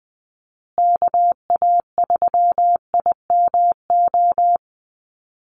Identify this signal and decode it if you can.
Morse code, heard as a keyed tone, about 20 wpm, 700 Hz. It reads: KA3IMO
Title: USA Call Signs Repeated 3x 20wpm